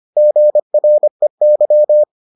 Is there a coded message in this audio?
Yes: Morse code from 0.2 to 2.0 s, heard as a keyed tone, about 25 wpm, 600 Hz. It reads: GREY